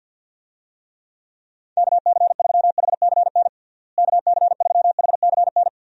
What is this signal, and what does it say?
Morse code, heard as a keyed tone, about 50 wpm, 700 Hz. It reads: KC3HCN KC3HCN
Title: USA Call Signs - 6 Characters - Repeated 2x - 50wpm